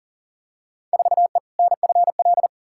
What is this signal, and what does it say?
Morse code, heard as a keyed tone, about 40 wpm, 700 Hz. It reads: 4E NFL
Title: Field Day Exchanges Encore - 40wpm